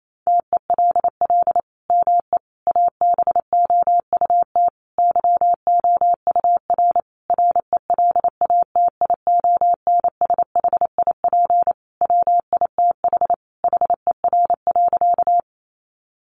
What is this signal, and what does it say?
Morse code, heard as a keyed tone, about 28 wpm, 700 Hz. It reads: TELL ME ABOUT YOUR RELATIONSHIP WITH HER.
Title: Sentences from Top 1100 Words 28wpm